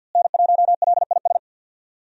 Morse code, heard as a keyed tone, about 52 wpm, 700 Hz. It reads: N1LIS
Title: USA Call Signs - 5 Characters - Repeated 3x - 50wpm